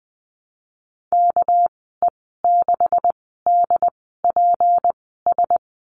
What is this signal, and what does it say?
Morse code, heard as a keyed tone, about 20 wpm, spaced as for 14 wpm, 700 Hz. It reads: KE6DPS
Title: 245 Lesson 39g - Letter z - Call Signs 20wpm